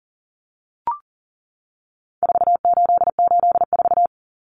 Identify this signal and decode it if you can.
Morse code, heard as a keyed tone, about 39 wpm, 700 Hz. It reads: 4884
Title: Common Cut Numbers - 4-Digit 40wpm